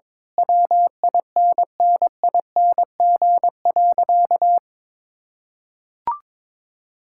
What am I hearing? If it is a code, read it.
Morse code, heard as a keyed tone, about 22 wpm, 700 Hz. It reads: WINNING.
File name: practice / 22wpm / Sentences with 8-Word Phrases - 22wpm